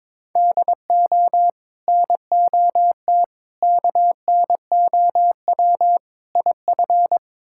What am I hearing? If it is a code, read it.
Morse code, heard as a keyed tone, about 22 wpm, 700 Hz. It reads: DO NOT KNOW IF
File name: practice / 22wpm / Build A Phrase - 5-Word Phrases - 22wpm